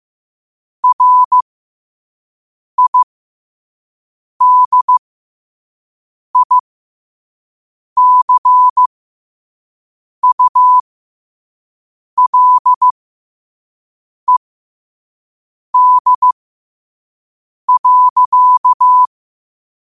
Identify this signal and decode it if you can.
Morse code, heard as a keyed tone, about 15 wpm, spaced as for 5 wpm, 1000 Hz. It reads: RIDICULED.
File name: qotdfiles / 5wpm